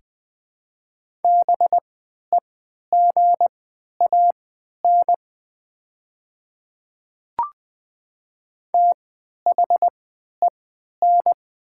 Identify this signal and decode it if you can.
Morse code, heard as a keyed tone, about 20 wpm, spaced as for 11 wpm, 700 Hz. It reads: BEGAN THEN
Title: 3x Character Spacing - Top 300 Words - 20wpm